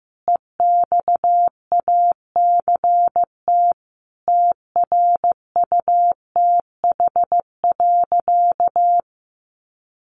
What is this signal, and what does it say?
Morse code, heard as a keyed tone, about 15 wpm, 700 Hz. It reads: EXACT TRUTH.
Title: Sentences from Top 1300 Words 15wpm